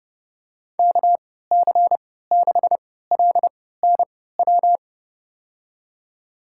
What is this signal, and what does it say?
Morse code, heard as a keyed tone, about 30 wpm, spaced as for 17 wpm, 700 Hz. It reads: KC6LNW
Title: USA Call Signs - 6 Characters - 3x Character Spacing - 30wpm